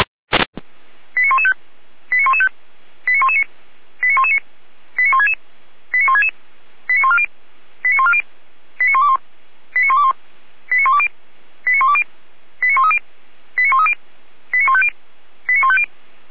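Five-tone selective call (ZVEI), heard as quick runs of five tones, tones of about 70 ms. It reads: 89115 89115 89119 89119 89166 89166 89140 89140 89121 89121 89120 89120 89130 89130 89150 89150
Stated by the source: ZVEI five-tone selective call 89084 received by SDR.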